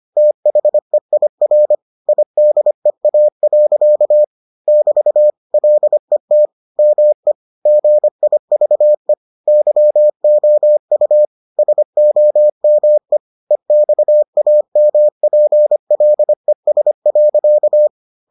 Morse code, heard as a keyed tone, about 25 wpm, 600 Hz. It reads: THEIR IDEA. <BT> LET ME GIVE YOU SOME EXAMPLES.